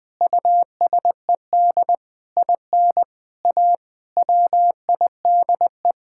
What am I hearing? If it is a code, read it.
Morse code, heard as a keyed tone, about 20 wpm, 700 Hz. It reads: USED IN A WIDE
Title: Sentences from Top 1000 Words - Review Part 6 - 20wpm